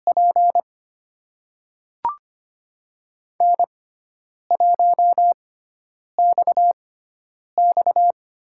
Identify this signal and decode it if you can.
Morse code, heard as a keyed tone, about 25 wpm, spaced as for 9 wpm, 700 Hz. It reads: P N1XX
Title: USA Call Signs - 5 Characters - 6x Character Spacing - 25wpm